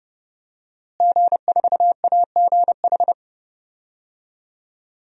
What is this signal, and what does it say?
Morse code, heard as a keyed tone, about 30 wpm, 700 Hz. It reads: G4AGH
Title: DXCC-100 Call Signs - 5 Characters - Repeated 3x - 30wpm